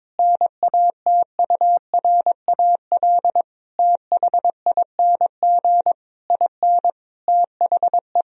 Morse code, heard as a keyed tone, about 22 wpm, 700 Hz. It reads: NATURAL THING IN THE
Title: Sentences from Top 1000 Words - Review Part 4 - 22wpm